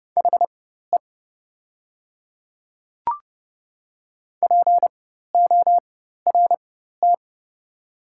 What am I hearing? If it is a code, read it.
Morse code, heard as a keyed tone, about 30 wpm, spaced as for 14 wpm, 690 Hz. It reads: HE PORT
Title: 4x Character Spacing - Top 300 Words - 30wpm